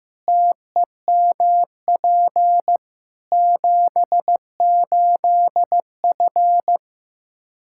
Morse code, heard as a keyed tone, about 15 wpm, 700 Hz. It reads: TEMP 78F
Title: K1USN SST QSO Practice 15wpm